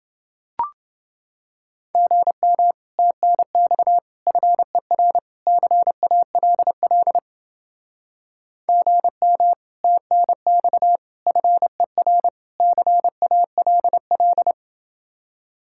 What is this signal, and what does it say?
Morse code, heard as a keyed tone, about 29 wpm, 700 Hz. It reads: GM TNX FER CALL GM TNX FER CALL